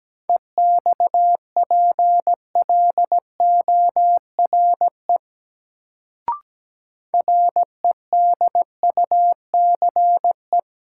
Morse code, heard as a keyed tone, about 17 wpm, 700 Hz. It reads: EXPLORE REDUCE
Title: Sentences from Top 1100 Words - New Word Drill - 17wpm